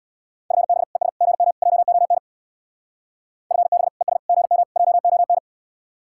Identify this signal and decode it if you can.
Morse code, heard as a keyed tone, about 65 wpm, 700 Hz. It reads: FB ES GM JON FB ES GM JON